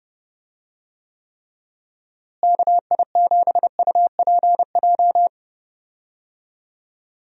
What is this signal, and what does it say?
Morse code, heard as a keyed tone, about 32 wpm, 700 Hz. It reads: KI7UPJ
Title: USA Call Signs - 6 Characters - Repeated 2x - 30wpm